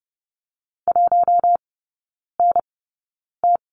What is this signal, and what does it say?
Morse code, heard as a keyed tone, about 30 wpm, spaced as for 9 wpm, 700 Hz. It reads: 1NT
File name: course / 30:10wpm / 087 Lesson 15e - Number 5 - Three Characters 30wpm